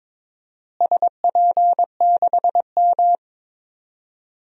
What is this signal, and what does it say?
Morse code, heard as a keyed tone, about 22 wpm, 700 Hz. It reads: SP6M
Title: DXCC-100 Call Signs - 4 Characters - Repeated 2x - 22wpm